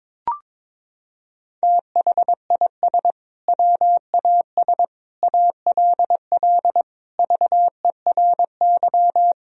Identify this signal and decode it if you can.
Morse code, heard as a keyed tone, about 22 wpm, 690 Hz. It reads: THIS WAS ALL VERY
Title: Sentences from Top 1000 Words - Review Part 3 - 22wpm